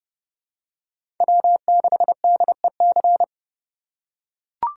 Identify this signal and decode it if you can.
Morse code, heard as a keyed tone, about 30 wpm, 700 Hz. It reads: W6DEC
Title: USA Call Signs - 5 Characters - 3x Character Spacing - 30wpm